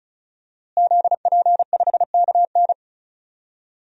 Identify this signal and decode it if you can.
Morse code, heard as a keyed tone, about 36 wpm, 700 Hz. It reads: ZP5KN
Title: DXCC-100 Call Signs - 5 Characters - Repeated 2x - 35wpm